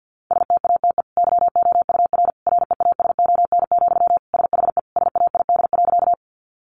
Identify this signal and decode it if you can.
Morse code, heard as a keyed tone, about 50 wpm, 700 Hz. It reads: STATE YOUR REASON, SHE SAID.
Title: Sentences from Top 700 Words 50wpm